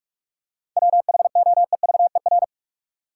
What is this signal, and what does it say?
Morse code, heard as a keyed tone, about 43 wpm, 700 Hz. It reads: WHOEVER
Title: Seven Letter Words - 45wpm